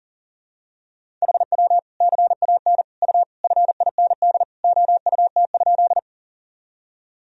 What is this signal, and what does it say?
Morse code, heard as a keyed tone, about 40 wpm, 700 Hz. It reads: HW CAN U FIND OUT?
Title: Sentences from Top 200 Words and Abbreviations 40wpm